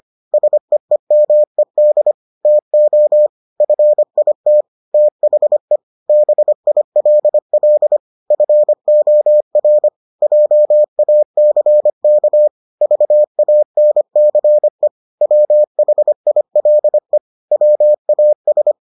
Morse code, heard as a keyed tone, about 25 wpm, 600 Hz. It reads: SEEMED TO FIT THE BILL FOR JACK VANCE WHILE WAS